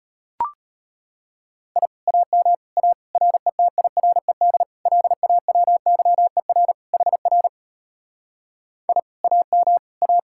Morse code, heard as a keyed tone, about 33 wpm, 700 Hz. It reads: I AM A RETIRED LAWYER HR I AM A